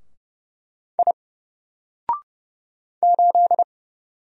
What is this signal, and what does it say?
Morse code, heard as a keyed tone, about 30 wpm, 700 Hz. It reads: I 8
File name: Single-Letter-Number-30wpm